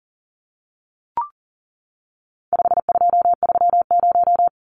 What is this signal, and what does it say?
Morse code, heard as a keyed tone, about 40 wpm, 700 Hz. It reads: 5230
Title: Numbers - 4-Digit 40wpm